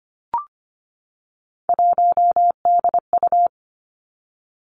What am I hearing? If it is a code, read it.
Morse code, heard as a keyed tone, about 25 wpm, 700 Hz. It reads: 1DU
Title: Three Challenging Letter-Number Combinations 25wpm